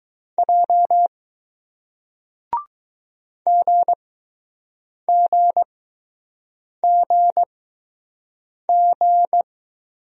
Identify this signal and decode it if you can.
Morse code, heard as a keyed tone, about 21 wpm, spaced as for 6 wpm, 700 Hz. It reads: J GGGG